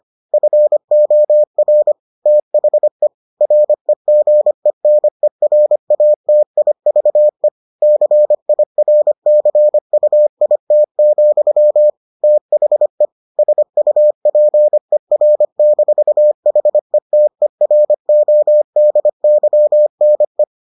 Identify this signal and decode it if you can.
Morse code, heard as a keyed tone, about 25 wpm, 600 Hz. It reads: FOR THE REGENERATIVE CIRCUIT, THE SUPER-HETERODYNE